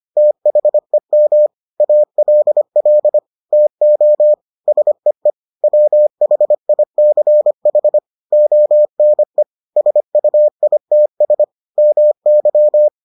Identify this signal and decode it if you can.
Morse code, heard as a keyed tone, about 25 wpm, 600 Hz. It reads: THEM ALL TO SEE WHICH ONE SUITS MY